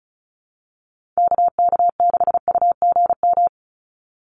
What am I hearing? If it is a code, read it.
Morse code, heard as a keyed tone, about 34 wpm, 700 Hz. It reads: KK6UGM
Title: USA Call Signs 35wpm